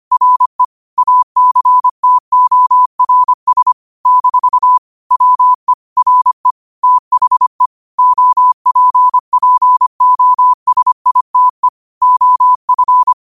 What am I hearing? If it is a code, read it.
Morse code, heard as a keyed tone, about 25 wpm, 1000 Hz. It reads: RE ACTORS - WE RE THE OPPOSITE OF
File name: qotdfiles / 25wpm